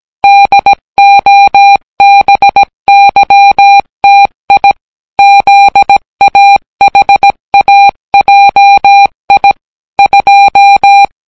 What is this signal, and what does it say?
Morse code, heard as a keyed tone, about 17 wpm, 800 Hz. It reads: DOBYTI ZAHAJI 2